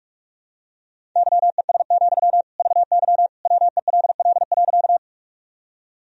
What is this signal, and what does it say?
Morse code, heard as a keyed tone, about 41 wpm, 700 Hz. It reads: YES, VY WELL.
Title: Sentences from Top 500 Words and Abbreviations 45wpm